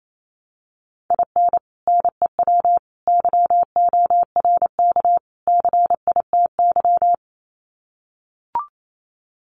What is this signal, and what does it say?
Morse code, heard as a keyed tone, about 28 wpm, 700 Hz. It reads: IN NEW YORK CITY
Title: Sets of 4 Words 28wpm